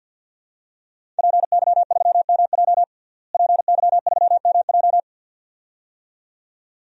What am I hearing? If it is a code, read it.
Morse code, heard as a keyed tone, about 52 wpm, 700 Hz. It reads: PY3MJ PY3MJ
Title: DXCC-100 Call Signs - 5 Characters - Repeated 2x - 50wpm